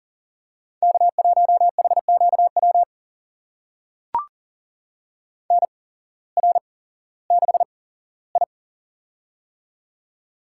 Morse code, heard as a keyed tone, about 40 wpm, spaced as for 19 wpm, 700 Hz. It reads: K1HQW NR6I
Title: USA Call Signs - 8x Character Spacing - 40wpm